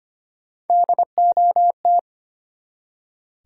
Morse code, heard as a keyed tone, about 25 wpm, 700 Hz. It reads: DOT
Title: Three Letter Words - 25wpm